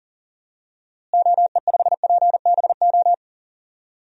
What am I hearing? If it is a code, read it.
Morse code, heard as a keyed tone, about 37 wpm, 700 Hz. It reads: OE5PBO
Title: DXCC-100 Call Signs - 6 Characters - Repeated 2x - 40wpm